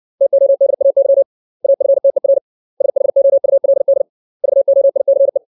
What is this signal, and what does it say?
Morse code, heard as a keyed tone, about 60 wpm, 550 Hz. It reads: TODAY AFTER SHOULD VOICE
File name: 632 words / 632 Words 60WPM